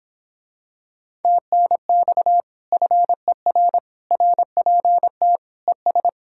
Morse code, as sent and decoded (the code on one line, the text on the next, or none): - -. -..- / ..-. .-. / .-. .--. - / . ...
TNX FER RPT ES